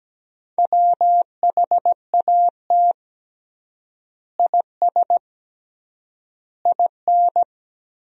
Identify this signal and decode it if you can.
Morse code, heard as a keyed tone, about 17 wpm, 700 Hz. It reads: WHAT IS IN
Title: 3x Word-Spacing - Sentences from Top 200 Words - 17wpm